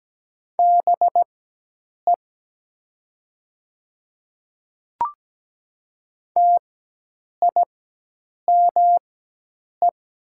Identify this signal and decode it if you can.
Morse code, heard as a keyed tone, about 17 wpm, spaced as for 8 wpm, 700 Hz. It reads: BE TIME
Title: 4x Character Spacing - Top 100 Words - 17wpm